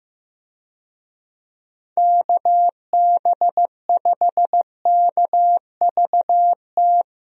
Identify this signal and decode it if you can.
Morse code, heard as a keyed tone, about 15 wpm, 700 Hz. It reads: KB5KVT
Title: USA Call Signs - 6 Characters - Repeated 3x - 15wpm